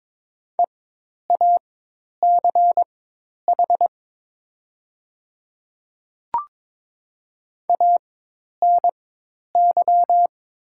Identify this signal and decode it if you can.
Morse code, heard as a keyed tone, about 22 wpm, spaced as for 10 wpm, 700 Hz. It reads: EACH ANY
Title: 4x Character Spacing - Top 200 Words - 22wpm